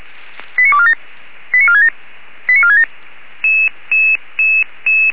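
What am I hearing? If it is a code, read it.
Five-tone selective call (ZVEI), heard as quick runs of five tones, tones of about 70 ms. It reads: 89247 89467 89467
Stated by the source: ZVEI five-tone selective call received with an SDR, from the Feuerwehr Orlamünde fire brigade audio server.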